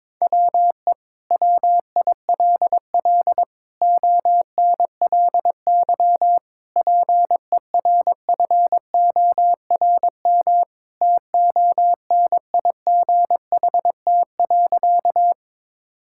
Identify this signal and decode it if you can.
Morse code, heard as a keyed tone, about 22 wpm, 700 Hz. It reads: WE WILL ONLY PERFORM TONIGHT.
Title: Sentences from Top 1000 Words - Review Part 5 - 22wpm